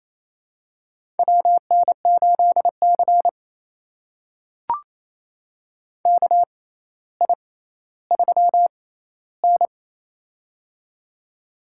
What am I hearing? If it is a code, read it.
Morse code, heard as a keyed tone, about 28 wpm, spaced as for 14 wpm, 700 Hz. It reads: WN8C KI3N
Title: USA Call Signs - 4 Characters - 6x Character Spacing - 28wpm